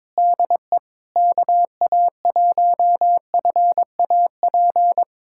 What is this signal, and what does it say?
Morse code, heard as a keyed tone, about 21 wpm, 700 Hz. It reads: DE KA1FAP
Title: USA Callsign DE Callsign - 6 Characters - 22wpm